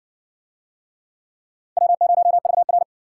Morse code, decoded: W0VR

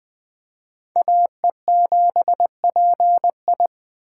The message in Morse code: AE7PI

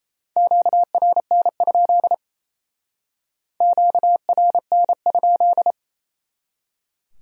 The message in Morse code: QRN? QRN?

